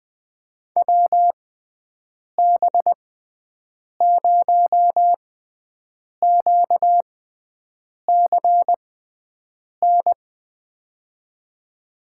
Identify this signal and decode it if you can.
Morse code, heard as a keyed tone, about 20 wpm, spaced as for 7 wpm, 700 Hz. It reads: WB0QCN
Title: USA Call Signs - 6x Character Spacing - 20wpm